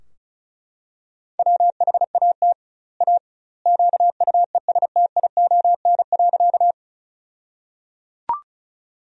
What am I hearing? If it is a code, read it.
Morse code, heard as a keyed tone, about 36 wpm, 700 Hz. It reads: WHAT A QUESTION.
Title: 2x Word-Spacing - Sentences from Top 400 Words - 35wpm